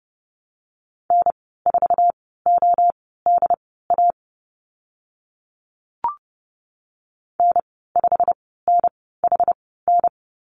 Morse code, heard as a keyed tone, about 30 wpm, spaced as for 17 wpm, 700 Hz. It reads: N4ODA N5NHN
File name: Lesson 15g - Number 5 - Call Signs 30wpm